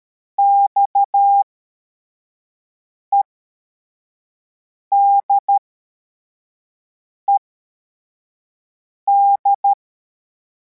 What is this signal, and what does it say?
Morse code, heard as a keyed tone, about 13 wpm, spaced as for 4 wpm, 800 Hz. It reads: XEDED
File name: formation / cw / Text21